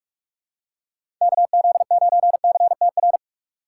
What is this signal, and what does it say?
Morse code, heard as a keyed tone, about 46 wpm, 700 Hz. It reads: KZ9CTR
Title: USA Call Signs - 6 Characters - 4x Character Spacing - 45wpm